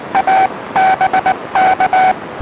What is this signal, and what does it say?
Morse code, heard as a keyed tone, about 20 wpm, 760 Hz, with noise. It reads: A BK
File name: satellite / AO51 CW